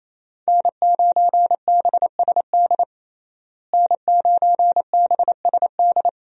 Morse code, heard as a keyed tone, about 28 wpm, 700 Hz. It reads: N9BSD N9BSD
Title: USA Call Signs - 5 Characters - Repeated 2x - 28wpm